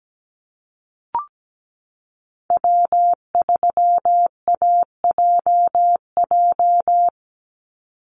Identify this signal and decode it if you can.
Morse code, heard as a keyed tone, about 17 wpm, 700 Hz. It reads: W3AJJ
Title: USA Call Signs 17wpm